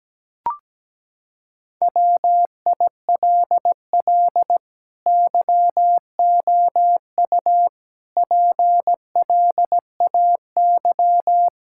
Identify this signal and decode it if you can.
Morse code, heard as a keyed tone, about 17 wpm, 700 Hz. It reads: WILL YOU PLAY
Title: Sentences from Top 1000 Words - Review Part 6 - 17wpm